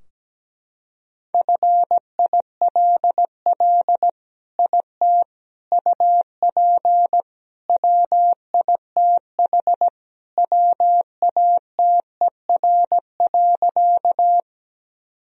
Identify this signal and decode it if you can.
Morse code, heard as a keyed tone, about 17 wpm, 700 Hz. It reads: FILL IT UP WITH WATER.
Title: Sentences from Top 1000 Words - Review Part 3 - 17wpm